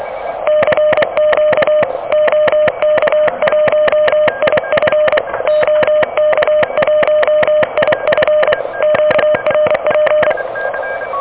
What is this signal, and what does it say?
Morse code, heard as a keyed tone, about 26 wpm, 600 Hz, with noise. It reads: CQ OK1IF OK1IF QRP
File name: OK QRP 2023 / OK QRP 2023